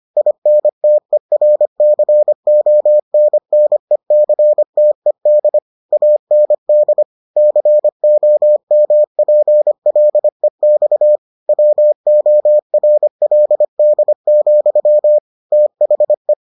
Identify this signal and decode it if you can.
Morse code, heard as a keyed tone, about 25 wpm, 600 Hz. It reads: INTERCONNECTED AND COMPLEX WORLD, THE